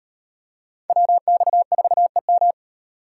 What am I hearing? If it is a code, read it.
Morse code, heard as a keyed tone, about 36 wpm, 700 Hz. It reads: WX4EM